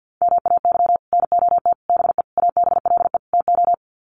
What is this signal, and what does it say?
Morse code, heard as a keyed tone, about 51 wpm, 700 Hz. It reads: MAY NOT BE ABLE TO
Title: Sets of 5 Words 50wpm